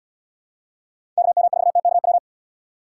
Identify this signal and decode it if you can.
Morse code, heard as a keyed tone, about 76 wpm, 700 Hz. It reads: KM4TKW